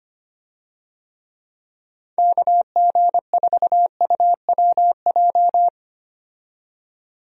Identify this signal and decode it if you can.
Morse code, heard as a keyed tone, about 25 wpm, 700 Hz. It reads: KG4UWJ
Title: USA Call Signs - 6 Characters - 25wpm